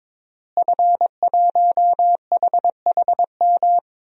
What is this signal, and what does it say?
Morse code, heard as a keyed tone, about 22 wpm, 700 Hz. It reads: F1HHM
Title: DXCC-100 Call Signs - 5 Characters - Repeated 2x - 22wpm